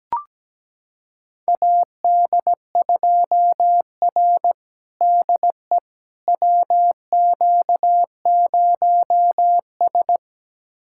Morse code, heard as a keyed tone, about 17 wpm, 700 Hz. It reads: AD2R DE WQ0S
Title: USA Callsign DE Callsign - 4 Characters - 17wpm